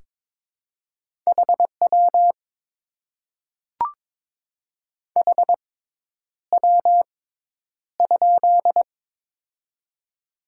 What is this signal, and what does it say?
Morse code, heard as a keyed tone, about 22 wpm, spaced as for 10 wpm, 700 Hz. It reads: HW HW?
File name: Top 90 QSO Elements - 6x Character Spacing - 22wpm